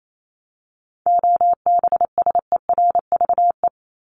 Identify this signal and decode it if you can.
Morse code, heard as a keyed tone, about 28 wpm, 700 Hz. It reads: OBSERVE
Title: ED Suffix - Top 100 Words with Base Words 28wpm